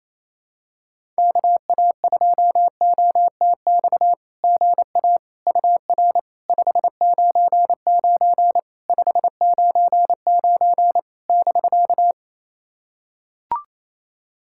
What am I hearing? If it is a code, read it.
Morse code, heard as a keyed tone, about 28 wpm, 700 Hz. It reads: KA2OTX GA UR 599 599 <BK>
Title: SOTA QSO Practice 28wpm